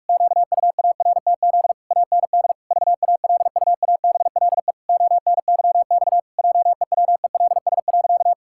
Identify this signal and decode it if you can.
Morse code, heard as a keyed tone, about 41 wpm, 700 Hz. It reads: QUARTZ AND VALUABLE ONYX JEWELS.
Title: Pangram Challenge v2 - 45wpm